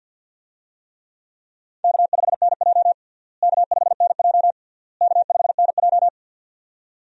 Morse code, heard as a keyed tone, about 52 wpm, 700 Hz. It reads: K5NJ K5NJ K5NJ
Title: USA Call Signs - 4 Characters - Repeated 3x - 50wpm